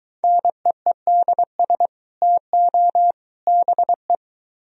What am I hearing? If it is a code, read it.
Morse code, heard as a keyed tone, about 22 wpm, 700 Hz. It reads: NEEDS TO BE